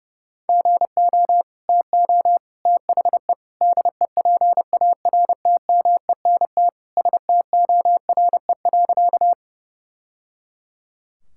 Morse code, heard as a keyed tone, about 32 wpm, 700 Hz. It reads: GO TO THE DEPARTMENT STORE.
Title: Sentences from Top 1000 Words - Review Part 5 - 30wpm